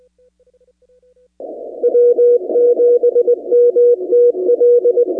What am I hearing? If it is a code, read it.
Morse code, heard as a keyed tone, about 20 wpm, 480 Hz. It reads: W7MTL
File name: 2019-cw-wpx-kt5j-k5pi